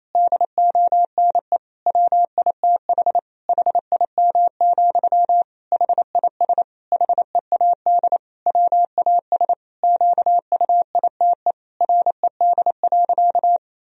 Morse code, heard as a keyed tone, about 28 wpm, 700 Hz. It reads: DONE WITH HIM, HIS HEAD WAS QUITE RED.
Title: Sentences from Top 1000 Words - Review Part 2 - 28wpm